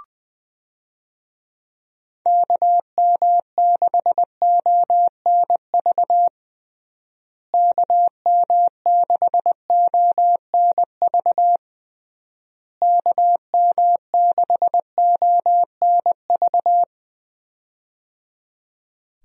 Morse code, heard as a keyed tone, about 20 wpm, 700 Hz. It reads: KM6ONV KM6ONV KM6ONV
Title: USA Call Signs - 6 Characters - Repeated 3x - 20wpm